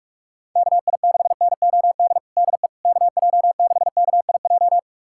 Morse code, heard as a keyed tone, about 46 wpm, 700 Hz. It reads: KI6NOD DE KJ6KIJ
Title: USA Callsign DE Callsign - 6 Characters - 45wpm